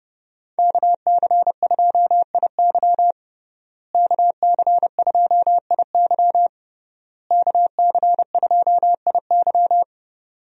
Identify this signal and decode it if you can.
Morse code, heard as a keyed tone, about 32 wpm, 700 Hz. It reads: KC2IY KC2IY KC2IY
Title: USA Call Signs - 5 Characters - Repeated 3x - 30wpm